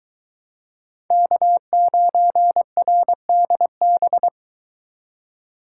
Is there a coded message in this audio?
Yes: Morse code from 1.1 to 4.3 s, heard as a keyed tone, about 23 wpm, 700 Hz. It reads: K9RDB